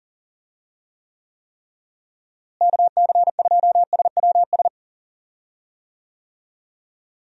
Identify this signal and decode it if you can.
Morse code, heard as a keyed tone, about 40 wpm, 700 Hz. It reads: KC2SWS
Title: USA Call Signs - 6 Characters - Repeated 3x - 40wpm